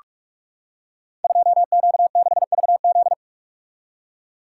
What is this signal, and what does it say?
Morse code, heard as a keyed tone, about 44 wpm, 700 Hz. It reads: 2Q6VZ
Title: Code Groups - 5 Letter:Numbers - 45wpm